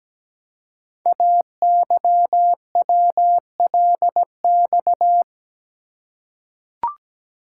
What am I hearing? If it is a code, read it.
Morse code, heard as a keyed tone, about 17 wpm, 700 Hz. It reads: AYWLX
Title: Code Groups - 5 Letters - 17wpm